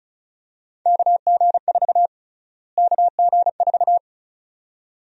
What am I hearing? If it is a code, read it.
Morse code, heard as a keyed tone, about 36 wpm, 700 Hz. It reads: KG4 KG4
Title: USA Call Sign Prefixes - Repeated 2x - 35wpm